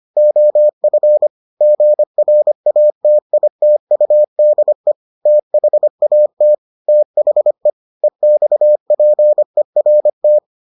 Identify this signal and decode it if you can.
Morse code, heard as a keyed tone, about 25 wpm, 600 Hz. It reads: OF GRATITUDE THAT THE EXPERT